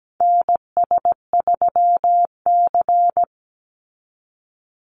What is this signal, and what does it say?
Morse code, heard as a keyed tone, about 17 wpm, 700 Hz. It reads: NS3C
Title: Call Signs Repeated 2x - 17wpm